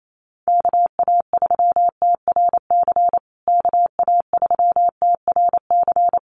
Morse code, heard as a keyed tone, about 28 wpm, 700 Hz. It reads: KA3TRC KA3TRC
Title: USA Call Signs Repeated 3x 28wpm